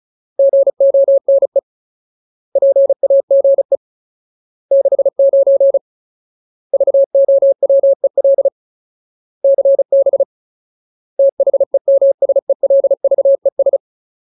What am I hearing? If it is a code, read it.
Morse code, heard as a keyed tone, about 34 wpm, 550 Hz. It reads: GONE PAGE 69 VOWEL CB THEMSELVES
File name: Ultimate Word List / 35WPM Ultimate